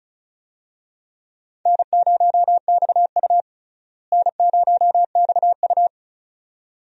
Morse code, heard as a keyed tone, about 36 wpm, 700 Hz. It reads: N0XU N0XU
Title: USA Call Signs - 4 Characters - Repeated 2x - 35wpm